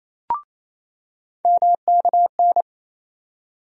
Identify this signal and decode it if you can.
Morse code, heard as a keyed tone, about 28 wpm, 700 Hz. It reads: MKN
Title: USA Call Sign Postfixes - 3 Letters 28wpm